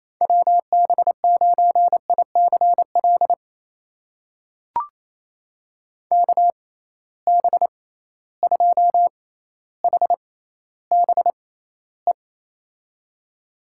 Morse code, heard as a keyed tone, about 28 wpm, spaced as for 14 wpm, 700 Hz. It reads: WB9ICL KB2HBE
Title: USA Call Signs - 6 Characters - 6x Character Spacing - 28wpm